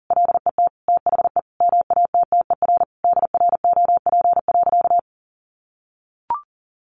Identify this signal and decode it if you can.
Morse code, heard as a keyed tone, about 40 wpm, 700 Hz. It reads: LET THE MATTER DROP.
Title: Sentences from Top 1400 Words 40wpm